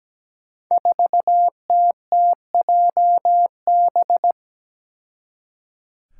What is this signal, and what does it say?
Morse code, heard as a keyed tone, about 17 wpm, 700 Hz. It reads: 4TTJB
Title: Five Challenging Letter-Number Combinations 17wpm